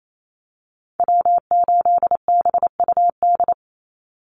Morse code, heard as a keyed tone, about 28 wpm, 700 Hz. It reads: W8BUD
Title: Call Signs Repeated 3x - 28wpm